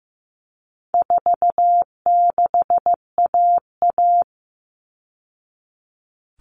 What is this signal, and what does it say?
Morse code, heard as a keyed tone, about 15 wpm, 700 Hz. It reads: 46AA
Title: All Cut Numbers - 4-Digit 15wpm